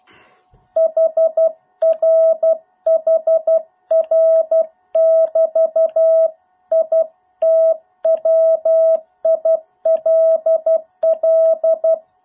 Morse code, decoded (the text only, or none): HR HR <BT> IT WILL